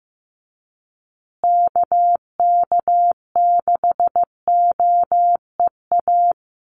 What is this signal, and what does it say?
Morse code, heard as a keyed tone, about 15 wpm, 700 Hz. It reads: KK6OEA
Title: USA Call Signs Repeated 2x 15wpm